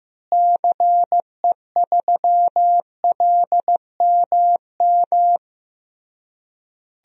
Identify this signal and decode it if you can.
Morse code, heard as a keyed tone, about 14 wpm, 700 Hz. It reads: CE3LMM
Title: DXCC-100 Call Signs - 6 Characters - Repeated 3x - 15wpm